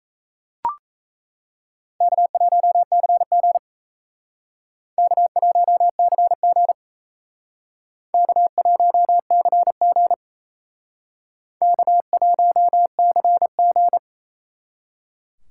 Morse code, heard as a keyed tone, about 35 wpm, 700 Hz. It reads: K1CG K1CG K1CG K1CG
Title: Speed Racing - USA 1x2 Call Signs 28wpm